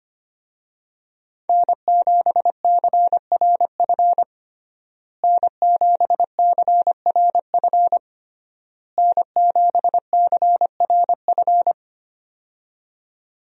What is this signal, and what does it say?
Morse code, heard as a keyed tone, about 25 wpm, 700 Hz. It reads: N7CRF N7CRF N7CRF
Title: USA Call Signs - 5 Characters - Repeated 3x - 25wpm